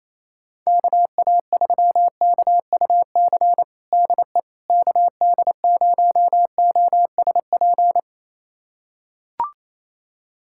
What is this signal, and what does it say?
Morse code, heard as a keyed tone, about 28 wpm, 700 Hz. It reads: KA3KUC DE KD0OSP
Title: USA Callsign DE Callsign 28wpm